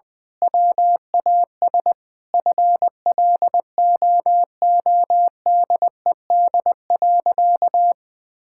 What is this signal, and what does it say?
Morse code, heard as a keyed tone, about 20 wpm, 700 Hz. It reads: WAS FLOODED.
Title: Sentences with 6-Word Phrases - Spoken Beginning - 20wpm